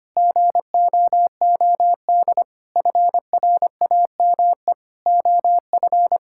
Morse code, heard as a keyed tone, about 25 wpm, 700 Hz. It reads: GOOD FRAME OF
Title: Sentences from Top 1000 Words - Review Part 6 - 25wpm